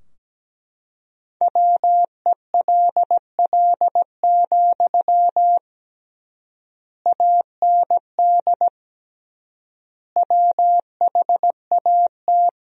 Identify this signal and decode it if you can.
Morse code, heard as a keyed tone, about 17 wpm, 700 Hz. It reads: WELL, AND WHAT
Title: 3x Word-Spacing - Sentences from Top 200 Words - 17wpm